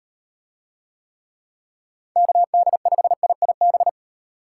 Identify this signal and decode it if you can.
Morse code, heard as a keyed tone, about 35 wpm, 700 Hz. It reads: KD5IIB